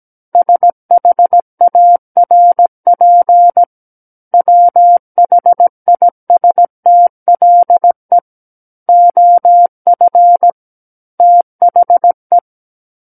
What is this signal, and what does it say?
Morse code, heard as a keyed tone, about 17 wpm, 700 Hz. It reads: SHARP WHISTLE OF THE
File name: Book One / 08-Chapter 8 - 17 WPM